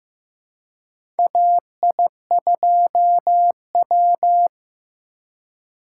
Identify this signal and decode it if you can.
Morse code, heard as a keyed tone, about 15 wpm, 700 Hz. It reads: AI2W
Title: USA Call Signs - 4 Characters - 2x Character Spacing - 15wpm